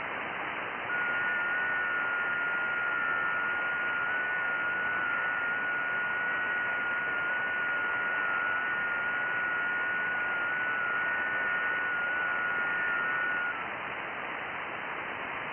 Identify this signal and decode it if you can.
FT8.
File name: IC-7100